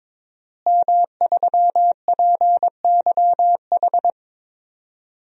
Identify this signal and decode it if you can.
Morse code, heard as a keyed tone, about 23 wpm, 700 Hz. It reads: M3PYH
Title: DXCC-100 Call Signs - 5 Characters - Repeated 3x - 22wpm